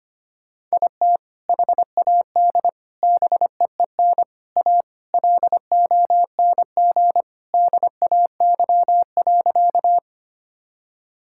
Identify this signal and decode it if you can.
Morse code, heard as a keyed tone, about 25 wpm, 700 Hz. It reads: IT HAD BEEN A LONG DAY.
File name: practice / 25wpm / Sentences with 3-Word Phrases - 25wpm